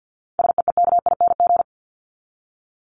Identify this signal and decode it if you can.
Morse code, heard as a keyed tone, about 50 wpm, 700 Hz. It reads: SEEKING
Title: ING Suffix - Top 100 Words with Base Words 50wpm